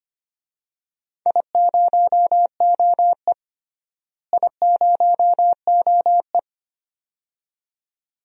Morse code, heard as a keyed tone, about 25 wpm, 700 Hz. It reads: I0OE I0OE
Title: DXCC-100 Call Signs - 4 Characters - Repeated 2x - 25wpm